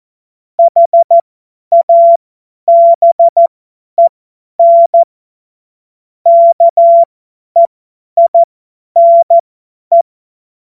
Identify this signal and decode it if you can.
Morse code, heard as a keyed tone, about 14 wpm, spaced as for 10 wpm, 680 Hz. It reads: HABEN KEINE
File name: Morsekurs / Witze / Witz 17.Oktober 2023 50 F70